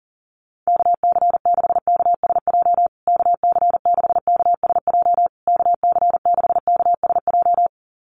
Morse code, heard as a keyed tone, about 40 wpm, 700 Hz. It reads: KC6KSJ KC6KSJ KC6KSJ
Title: USA Call Signs Repeated 3x 40wpm